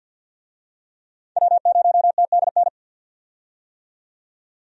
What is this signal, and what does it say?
Morse code, heard as a keyed tone, about 52 wpm, 700 Hz. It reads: W0TDN
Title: USA Call Signs - 5 Characters - 50wpm